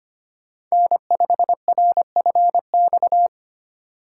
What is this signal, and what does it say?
Morse code, heard as a keyed tone, about 25 wpm, 700 Hz. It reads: N5RFX
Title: USA Call Signs - 5 Characters - 2x Character Spacing - 25wpm